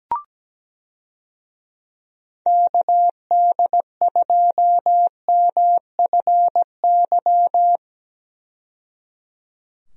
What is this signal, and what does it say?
Morse code, heard as a keyed tone, about 17 wpm, 700 Hz. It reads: KD2MFY
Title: USA Call Signs - 6 Characters - 17wpm